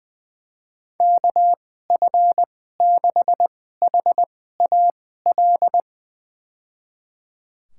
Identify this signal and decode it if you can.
Morse code, heard as a keyed tone, about 21 wpm, spaced as for 15 wpm, 700 Hz. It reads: KF6HAL
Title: USA Call Signs - 6 Characters - 2x Character Spacing - 20wpm